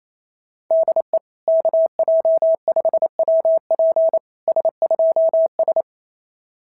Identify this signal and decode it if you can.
Morse code, heard as a keyed tone, about 29 wpm, 660 Hz. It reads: DE KJ5WP S2S
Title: SOTA QSO Practice 28wpm